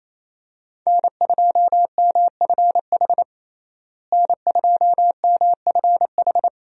Morse code, heard as a keyed tone, about 28 wpm, 700 Hz. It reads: N2MFH N2MFH
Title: USA Call Signs - 5 Characters - Repeated 2x - 28wpm